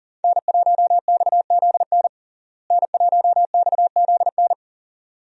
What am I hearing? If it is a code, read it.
Morse code, heard as a keyed tone, about 39 wpm, 700 Hz. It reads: N1XZN N1XZN
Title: USA Call Signs - 5 Characters - Repeated 2x - 40wpm